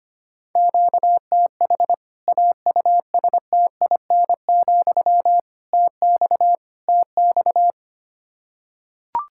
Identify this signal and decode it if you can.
Morse code, heard as a keyed tone, about 25 wpm, 700 Hz. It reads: QTH AUSTIN, TX TX
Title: K1USN SST QSO Practice 25wpm